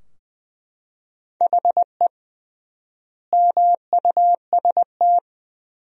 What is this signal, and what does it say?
Morse code, heard as a keyed tone, about 21 wpm, 700 Hz. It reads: HE MUST
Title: 3x Word-Spacing - Sentences from Top 500 Words - 20wpm